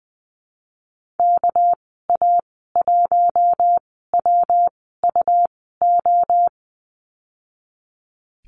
Morse code, heard as a keyed tone, about 20 wpm, spaced as for 15 wpm, 700 Hz. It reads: KA1WUO
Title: 207 Lesson 33g - Letter j - Call Signs 20wpm